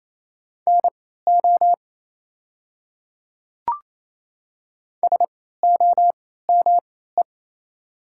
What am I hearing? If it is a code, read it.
Morse code, heard as a keyed tone, about 28 wpm, spaced as for 16 wpm, 700 Hz. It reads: NO SOME
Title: 3x Character Spacing - Top 100 Words - 28wpm